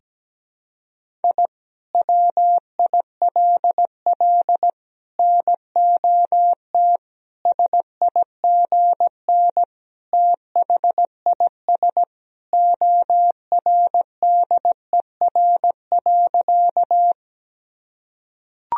Morse code, heard as a keyed tone, about 17 wpm, 700 Hz. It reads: I WILL NOT SIGN THIS ORDER.
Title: Sentences from Top 1000 Words - Review Part 2 - 17wpm